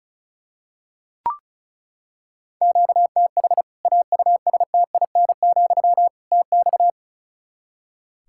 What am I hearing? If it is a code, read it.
Morse code, heard as a keyed tone, about 36 wpm, 700 Hz. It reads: QTH AUSTIN, TX
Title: Practice QSO - Exchange 1 - Beginning - 35wpm